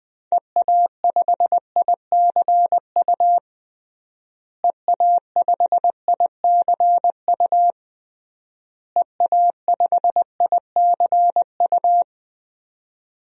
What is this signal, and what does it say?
Morse code, heard as a keyed tone, about 20 wpm, 700 Hz. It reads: EA5ICU EA5ICU EA5ICU
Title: DXCC-100 Call Signs - 6 Characters - Repeated 3x - 20wpm